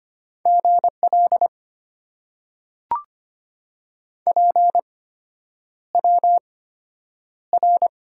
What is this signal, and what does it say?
Morse code, heard as a keyed tone, about 25 wpm, spaced as for 9 wpm, 690 Hz. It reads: GL PWR
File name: Top 90 QSO Elements - 8x Character Spacing - 25wpm